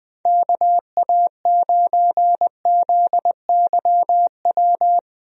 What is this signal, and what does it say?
Morse code, heard as a keyed tone, about 20 wpm, 700 Hz. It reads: KA9ZYW
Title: USA Call Signs - 6 Characters - Repeated 3x - 20wpm